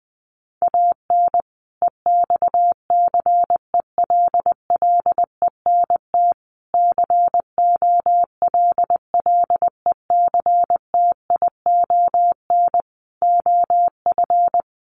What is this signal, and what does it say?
Morse code, heard as a keyed tone, about 20 wpm, 700 Hz. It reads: AN EXCELLENT COLLECTION OF
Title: Sentences from Top 1200 Words 20wpm